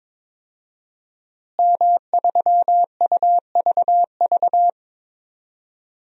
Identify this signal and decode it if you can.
Morse code, heard as a keyed tone, about 22 wpm, 700 Hz. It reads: M3UVV
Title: DXCC-100 Call Signs - 5 Characters - Repeated 3x - 22wpm